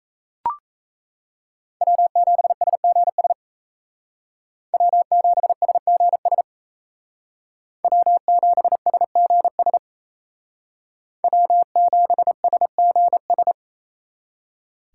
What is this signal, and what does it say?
Morse code, heard as a keyed tone, about 36 wpm, 700 Hz. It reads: W7SGS W7SGS W7SGS W7SGS